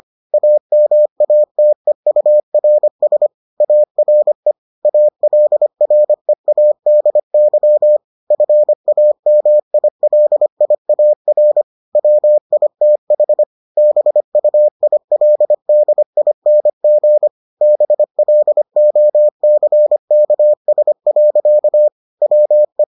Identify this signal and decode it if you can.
Morse code, heard as a keyed tone, about 25 wpm, 600 Hz. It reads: AMATEURS ARE ALREADY FAMILIAR WITH BUILDING BLOCKS. WE